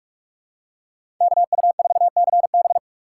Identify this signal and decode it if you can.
Morse code, heard as a keyed tone, about 46 wpm, 700 Hz. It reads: KU4CB
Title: USA Call Signs - 5 Characters - 4x Character Spacing - 45wpm